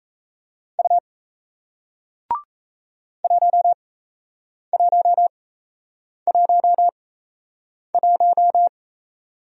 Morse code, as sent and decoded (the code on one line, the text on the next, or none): ..- / .---- .---- .---- .----
U 1111